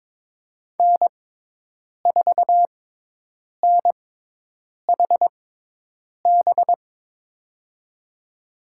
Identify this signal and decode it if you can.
Morse code, heard as a keyed tone, about 22 wpm, spaced as for 8 wpm, 700 Hz. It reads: N4NHB
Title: USA Call Signs - 6x Character Spacing - 22wpm